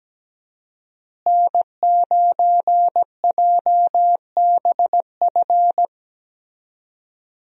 Morse code, heard as a keyed tone, about 17 wpm, 700 Hz. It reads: N9JBF
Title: USA Call Signs - 5 Characters - Repeated 2x - 17wpm